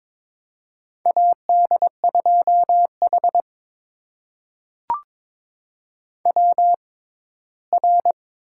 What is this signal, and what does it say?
Morse code, heard as a keyed tone, about 22 wpm, spaced as for 15 wpm, 700 Hz. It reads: AD2H WR
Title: USA Call Signs - 4 Characters - 6x Character Spacing - 22wpm